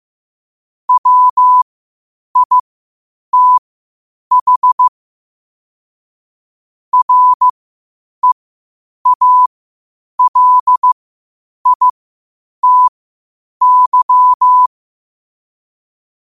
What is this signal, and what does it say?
Morse code, heard as a keyed tone, about 15 wpm, spaced as for 8 wpm, 1000 Hz. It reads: WITH REALITY
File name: qotdfiles / 8wpm